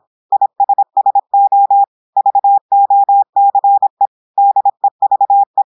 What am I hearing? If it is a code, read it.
Morse code, heard as a keyed tone, about 26 wpm, 800 Hz. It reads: ISSO VOCE DEVE
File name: portugais texte en clair 2 vitesse 26 mots minutes